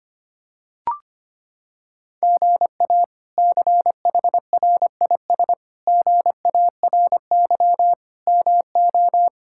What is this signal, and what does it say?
Morse code, heard as a keyed tone, about 25 wpm, 700 Hz. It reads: GA CHRIS GARY MO
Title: K1USN SST QSO Practice - 25wpm